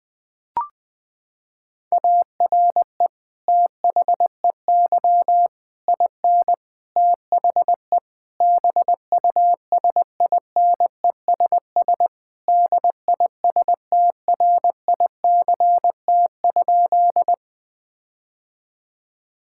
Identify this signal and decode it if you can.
Morse code, heard as a keyed tone, about 20 wpm, 700 Hz. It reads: ARE THEY IN THE BUSINESS DISTRICT?
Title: Sentences from Top 1600 Words 20wpm